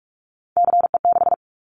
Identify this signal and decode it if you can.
Morse code, heard as a keyed tone, about 46 wpm, 700 Hz. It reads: CE6
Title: Callsign Prefixes for Top 100 DXCC - 45wpm